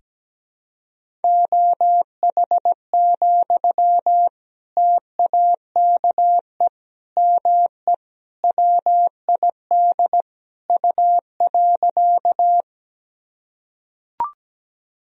Morse code, heard as a keyed tone, about 17 wpm, 700 Hz. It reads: OH, TAKE ME WID U.
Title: Sentences from Top 500 Words and Abbreviations 17wpm